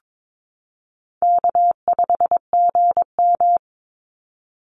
Morse code, heard as a keyed tone, about 22 wpm, 700 Hz. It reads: K5GM
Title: Easy Call Signs - 22wpm